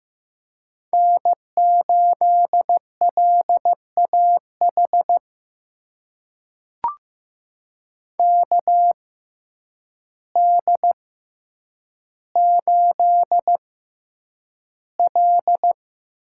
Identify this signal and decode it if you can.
Morse code, heard as a keyed tone, about 15 wpm, spaced as for 8 wpm, 700 Hz. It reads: N8LAH KD8L
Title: USA Call Signs - 6x Character Spacing - 15wpm